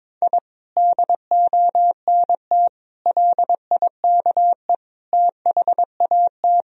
Morse code, heard as a keyed tone, about 21 wpm, 700 Hz. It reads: I DONT LIKE THAT
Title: Sentences from Top 1000 Words - Review Part 1 - 22wpm